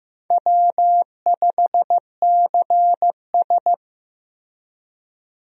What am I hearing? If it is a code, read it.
Morse code, heard as a keyed tone, about 15 wpm, 700 Hz. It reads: W5CS
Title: USA Call Signs - 4 Characters - Repeated 2x - 15wpm